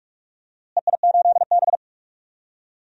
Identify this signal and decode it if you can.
Morse code, heard as a keyed tone, about 46 wpm, 700 Hz. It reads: EI8B
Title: DXCC-100 Call Signs - 4 Characters - Repeated 3x - 45wpm